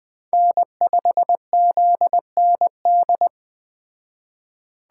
Morse code, heard as a keyed tone, about 20 wpm, 700 Hz. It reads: N5ZND
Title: Speed Racing - USA 1x3 Call Signs 17wpm